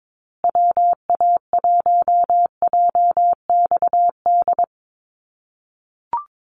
Morse code, heard as a keyed tone, about 22 wpm, 700 Hz. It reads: WA1JXD
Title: USA Call Signs Repeated 2x 22wpm